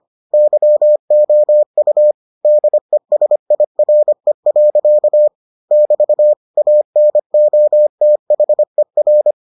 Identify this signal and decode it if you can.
Morse code, heard as a keyed tone, about 25 wpm, 600 Hz. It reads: YOU DESIRE. <BT> ANOTHER